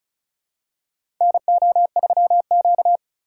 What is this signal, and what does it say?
Morse code, heard as a keyed tone, about 36 wpm, 700 Hz. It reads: NO3Q
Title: USA Call Signs - 4 Characters - 3x Character Spacing - 35wpm